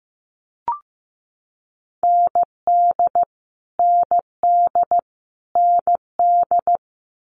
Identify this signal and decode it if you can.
Morse code, heard as a keyed tone, about 15 wpm, 700 Hz. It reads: ND ND ND
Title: Multi-letter Phonemes Repeated 3x 15wpm